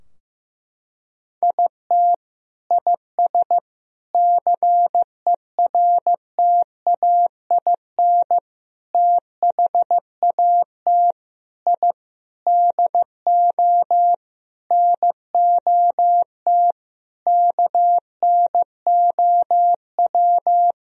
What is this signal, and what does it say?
Morse code, heard as a keyed tone, about 15 wpm, 700 Hz. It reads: IT IS CERTAIN THAT I DO NOT KNOW
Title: Sentences from Top 1000 Words - Review Part 2 - 15wpm